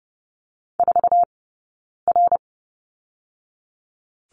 Morse code, heard as a keyed tone, about 30 wpm, 700 Hz. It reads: 4 R
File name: 092 Lesson 16d - Letter c - Two Characters 30wpm